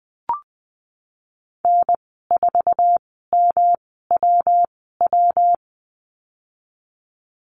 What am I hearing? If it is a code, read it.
Morse code, heard as a keyed tone, about 20 wpm, spaced as for 15 wpm, 700 Hz. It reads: N4MWW
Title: 154 Lesson 25g - Letter p - Call Signs 20wpm